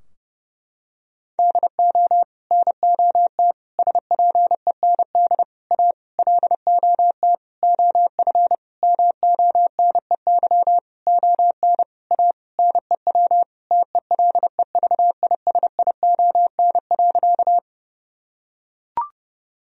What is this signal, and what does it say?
Morse code, heard as a keyed tone, about 30 wpm, 700 Hz. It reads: DO NOT SPEND A LOT OF MONEY ON A NEW TELEVISION.
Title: Sentences from Top 1000 Words - Review Part 4 - 30wpm